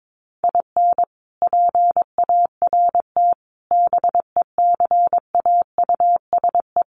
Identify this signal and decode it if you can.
Morse code, heard as a keyed tone, about 22 wpm, 700 Hz. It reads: IN PART BECAUSE
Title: Sets of 5 Words Encore2 22wpm